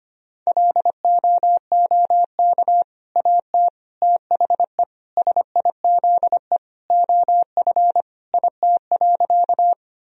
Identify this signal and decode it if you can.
Morse code, heard as a keyed tone, about 25 wpm, 700 Hz. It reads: LOOK AT THE SIZE OF IT.
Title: Sentences from Top 1000 Words - Review Part 2 - 25wpm